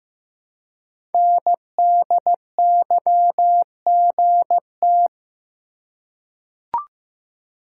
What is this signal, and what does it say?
Morse code, heard as a keyed tone, about 15 wpm, 700 Hz. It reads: NDYGT
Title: Code Groups - 5 Letter:Numbers - 15wpm